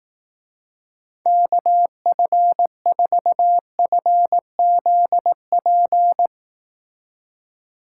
Morse code, heard as a keyed tone, about 18 wpm, 700 Hz. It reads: KF4FZP
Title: Speed Racing - USA 2x3 Call Signs 15wpm